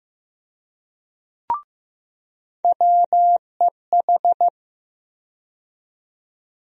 Morse code, heard as a keyed tone, about 15 wpm, 700 Hz. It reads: WEH
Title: USA Call Sign Postfixes - 3 Letters 15wpm